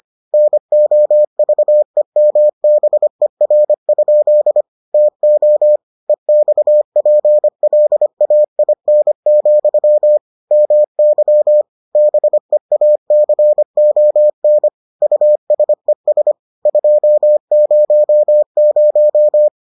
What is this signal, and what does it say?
Morse code, heard as a keyed tone, about 25 wpm, 600 Hz. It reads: NOVEMBER? TO EXPLAIN, MY BEACON USES 200